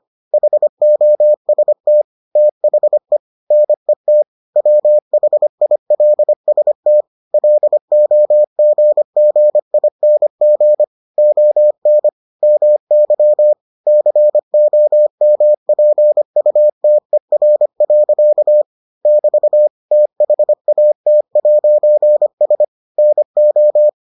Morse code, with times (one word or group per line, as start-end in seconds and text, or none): 0.3-2.0 HOST
2.4-3.2 THE
3.5-4.2 NET
4.6-7.0 WHILST
7.3-10.8 LOGGING
11.2-12.1 ON
12.4-13.5 MY
13.9-18.6 COMPUTER.
19.1-19.7 <BT>
19.9-22.6 THAT'S
23.0-23.9 NO